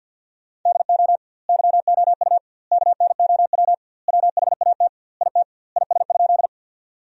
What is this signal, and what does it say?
Morse code, heard as a keyed tone, about 50 wpm, 700 Hz. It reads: DO YOU KNOW WHAT IT IS?
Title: 2x Word-Spacing - Sentences from Top 100 Words - 50wpm